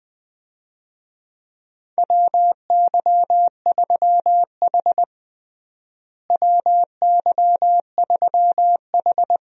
Morse code, heard as a keyed tone, about 20 wpm, 700 Hz. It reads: WY3H WY3H
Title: USA Call Signs - 4 Characters - Repeated 2x - 20wpm